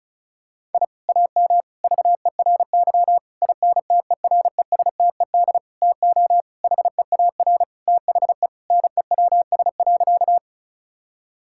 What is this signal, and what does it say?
Morse code, heard as a keyed tone, about 34 wpm, 700 Hz. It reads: I AM VERY INTERESTED TO HEAR THE NEWS.
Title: Sentences from Top 1000 Words - Review Part 6 - 35wpm